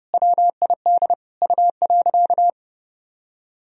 Morse code, heard as a keyed tone, about 29 wpm, 700 Hz. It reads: WID U.
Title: Sentences from Top 300 Words and Abbreviations 30wpm